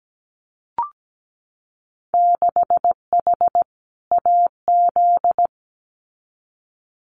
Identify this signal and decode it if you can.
Morse code, heard as a keyed tone, about 17 wpm, 700 Hz. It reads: 6H AZ
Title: Winter Field Day Exchanges Encore 17wpm